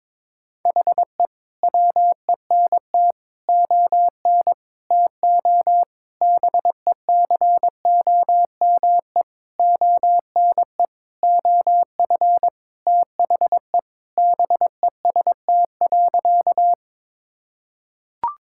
Morse code, heard as a keyed tone, about 22 wpm, 700 Hz. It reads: HE WENT ON TO BECOME ONE OF THE BEST.
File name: practice / 22wpm / Sentences with 7-Word Phrases - 22wpm